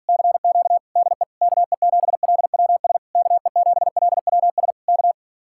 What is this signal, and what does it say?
Morse code, heard as a keyed tone, about 46 wpm, 700 Hz. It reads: CQ DE KE7LWS KE7LWS K